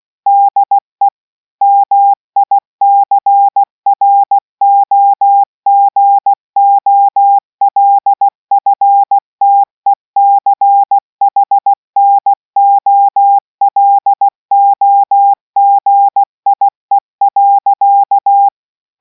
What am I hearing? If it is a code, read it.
Morse code, heard as a keyed tone, about 16 wpm, 800 Hz. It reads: DE MICROGOLFTECHNOLOGIE.